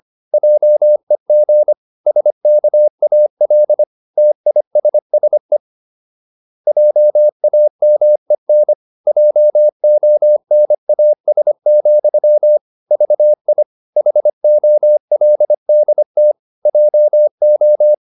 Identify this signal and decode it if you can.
Morse code, heard as a keyed tone, about 25 wpm, 600 Hz. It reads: JEG SKAL TISSE JAMEN JONAS, VI HOLDT JO
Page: Ferie 25 wpm